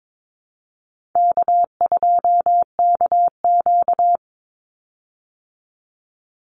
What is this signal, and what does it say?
Morse code, heard as a keyed tone, about 22 wpm, 700 Hz. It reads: K2KQ
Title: Easy Call Signs Repeated 2x - 22wpm